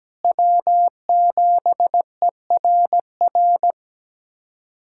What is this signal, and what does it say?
Morse code, heard as a keyed tone, about 17 wpm, 700 Hz. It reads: W7ERR
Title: USA Callsign DE Callsign - 5 Characters - 17wpm